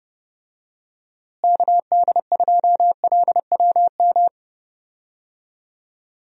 Morse code, heard as a keyed tone, about 31 wpm, 700 Hz. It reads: KD2LWM
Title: USA Call Signs - 6 Characters - Repeated 3x - 30wpm